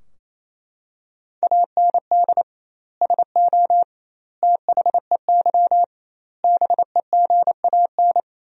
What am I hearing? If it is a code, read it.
Morse code, heard as a keyed tone, about 28 wpm, 700 Hz. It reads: AND SO THEY BEGAN
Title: 2x Word-Spacing - Sentences from Top 400 Words - 28wpm